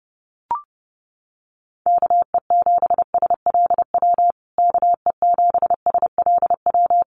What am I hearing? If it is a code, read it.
Morse code, heard as a keyed tone, about 30 wpm, 700 Hz. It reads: KE7SLW KE7SLW
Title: USA Call Signs Repeated 2x 30wpm